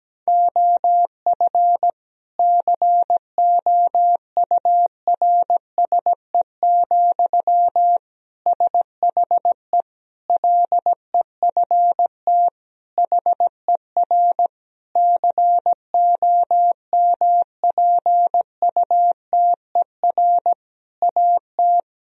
Morse code, heard as a keyed tone, about 17 wpm, 700 Hz. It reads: OF COURSE, SHE LEFT HER COMPUTER AT
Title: Sentences from Top 1000 Words - Review Part 3 - 17wpm